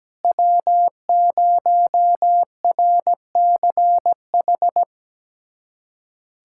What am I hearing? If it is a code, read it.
Morse code, heard as a keyed tone, about 17 wpm, 700 Hz. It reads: W0RCH
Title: USA Callsign DE Callsign - 5 Characters - 17wpm